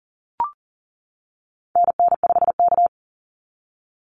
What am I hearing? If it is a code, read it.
Morse code, heard as a keyed tone, about 40 wpm, 700 Hz. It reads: NN5K
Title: NA SOTA Activator Callsigns 40wpm